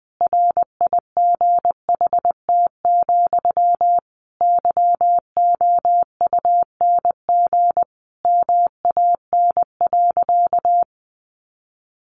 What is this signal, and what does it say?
Morse code, heard as a keyed tone, about 19 wpm, 700 Hz. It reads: RIGHT, YOUNG MAN.
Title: Sentences from Top 400 Words 20wpm